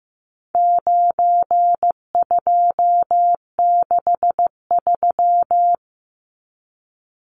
Morse code, heard as a keyed tone, about 15 wpm, 700 Hz. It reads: 9263
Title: Numbers - 4-Digit 15wpm